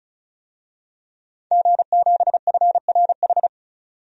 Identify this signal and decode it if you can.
Morse code, heard as a keyed tone, about 34 wpm, 700 Hz. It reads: G7FRH
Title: DXCC-100 Call Signs - 5 Characters - 35wpm